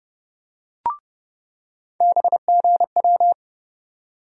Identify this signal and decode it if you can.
Morse code, heard as a keyed tone, about 31 wpm, 700 Hz. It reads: BGW
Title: USA Call Sign Postfixes - 3 Letters 30wpm